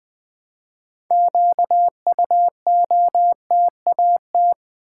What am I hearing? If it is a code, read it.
Morse code, heard as a keyed tone, about 20 wpm, 700 Hz. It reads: QUOTAT
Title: Build A Word - 3 Letters At A Time - 9-Letter Words - 20wpm